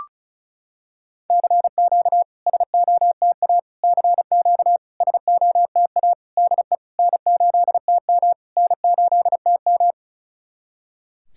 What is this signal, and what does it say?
Morse code, heard as a keyed tone, about 34 wpm, 690 Hz. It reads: CQ SOTA CQ SOTA DE N8TM N8TM
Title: SOTA QSO Practice 35wpm